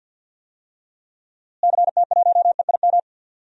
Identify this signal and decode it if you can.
Morse code, heard as a keyed tone, about 52 wpm, 700 Hz. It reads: CT1EIM
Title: DXCC-100 Call Signs - 6 Characters - Repeated 3x - 50wpm